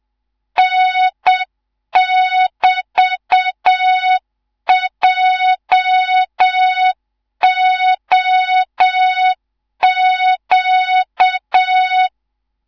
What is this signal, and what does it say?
Morse code, heard as a keyed tone, about 7 wpm, 3700 Hz, with noise. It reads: N<BT>JOQ
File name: Morze-HA2MN / hang / 19 GYAKORLAT